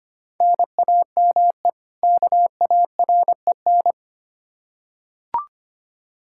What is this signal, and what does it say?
Morse code, heard as a keyed tone, about 25 wpm, 700 Hz. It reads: NAME KAREN
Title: Practice QSO - Exchange 1 - Beginning - 25wpm